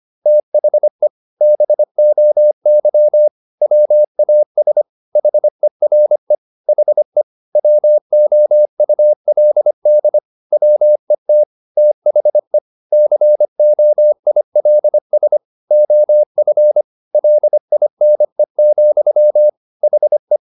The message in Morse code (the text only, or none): THE BOY WAS HERE HE WOULD WET THE COILS OF LINE, HE